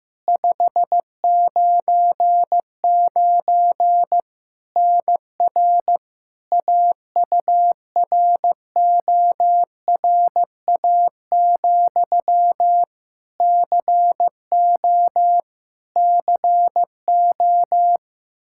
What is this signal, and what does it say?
Morse code, heard as a keyed tone, about 15 wpm, 700 Hz. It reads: 599 NR AURORA, CO CO
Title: K1USN SST QSO Practice 15wpm